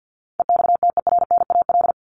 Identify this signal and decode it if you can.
Morse code, heard as a keyed tone, about 50 wpm, 700 Hz. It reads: EXTERNAL
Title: Top 1751-2000 Words 50wpm